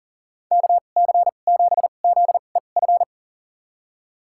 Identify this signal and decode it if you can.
Morse code, heard as a keyed tone, about 40 wpm, spaced as for 29 wpm, 700 Hz. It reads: KC7ZEF
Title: USA Call Signs - 6 Characters - 2x Character Spacing - 40wpm